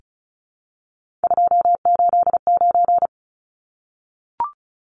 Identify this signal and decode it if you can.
Morse code, heard as a keyed tone, about 34 wpm, 700 Hz. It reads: 289
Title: Numbers - 3-Digit 35wpm